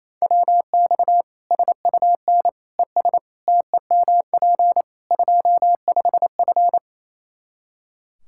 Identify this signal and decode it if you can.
Morse code, heard as a keyed tone, about 28 wpm, 700 Hz. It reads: WX SUN ES TEMP 25F
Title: Practice QSO - Exchange 2 - Info - 28wpm